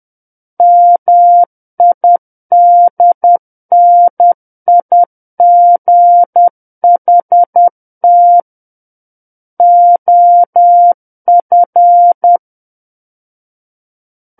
Morse code, heard as a keyed tone, about 10 wpm, 700 Hz. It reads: MIDNIGHT OF
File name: Book One / 01-Chapter 1 - 10 WPM